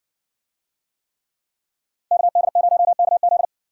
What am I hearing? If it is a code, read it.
Morse code, heard as a keyed tone, about 60 wpm, 700 Hz. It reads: KD0KZ